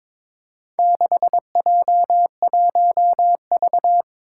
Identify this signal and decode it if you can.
Morse code, heard as a keyed tone, about 22 wpm, 700 Hz. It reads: 6J1V
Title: Four Challenging Letter-Number Combinations 22wpm